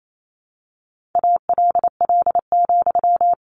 Morse code, heard as a keyed tone, about 28 wpm, 700 Hz. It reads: ALL,
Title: A Sound Of Thunder 28wpm